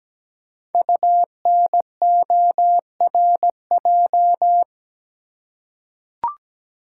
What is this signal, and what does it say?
Morse code, heard as a keyed tone, about 17 wpm, 700 Hz. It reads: UNORJ
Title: Code Groups - 5 Letters - 17wpm